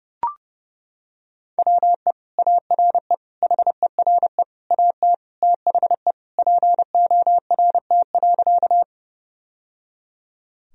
Morse code, heard as a keyed tone, about 30 wpm, 700 Hz. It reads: WE ARE HERE AT THE PORT.
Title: Sentences from Top 1000 Words - Review Part 1 - 30wpm